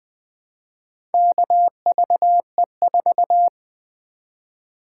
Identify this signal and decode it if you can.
Morse code, heard as a keyed tone, about 20 wpm, 700 Hz. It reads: KVE4
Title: Four Letter-Number Combinations - 20wpm